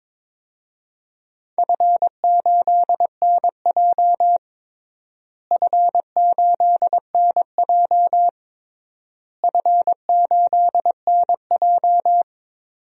Morse code, heard as a keyed tone, about 21 wpm, 700 Hz. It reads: F8NJ F8NJ F8NJ
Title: DXCC-100 Call Signs - 4 Characters - Repeated 3x - 22wpm